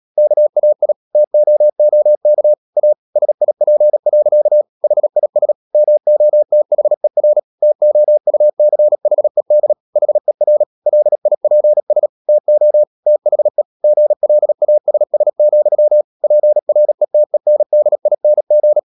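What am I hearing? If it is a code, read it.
Morse code, heard as a keyed tone, about 37 wpm, 600 Hz. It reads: KAI TOOK A SIP. HIS MOTHER TOUCHED HER LIPS TO THE GLASS, PRETENDING
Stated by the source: Morse code practice at 35–39 wpm sending English text of Ken Follett's novel Never, from CW med Gnister.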